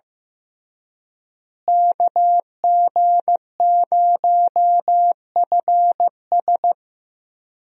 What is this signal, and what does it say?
Morse code, heard as a keyed tone, about 15 wpm, 700 Hz. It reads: KG0FS
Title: USA Call Signs - 5 Characters - Repeated 2x - 15wpm